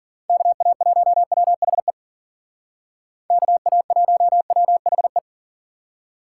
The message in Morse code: KA1WHE KA1WHE

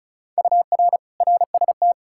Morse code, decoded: UR RST

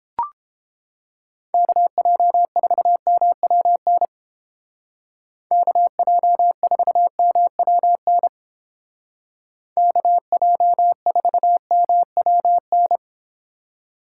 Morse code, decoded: KJ4MWN KJ4MWN KJ4MWN

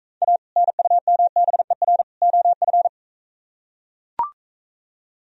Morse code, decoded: A NUMBER OF

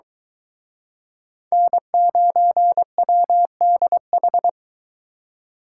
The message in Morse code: N9WDH

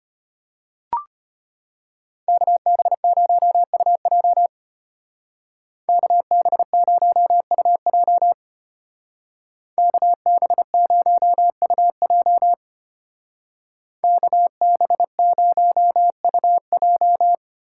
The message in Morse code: KB0UJ KB0UJ KB0UJ KB0UJ

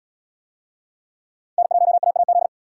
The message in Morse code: N1AAZ

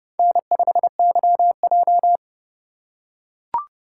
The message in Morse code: N5YJ

